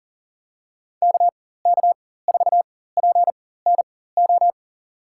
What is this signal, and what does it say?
Morse code, heard as a keyed tone, about 40 wpm, spaced as for 19 wpm, 700 Hz. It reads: KK4PNO
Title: USA Call Signs - 6 Characters - 4x Character Spacing - 40wpm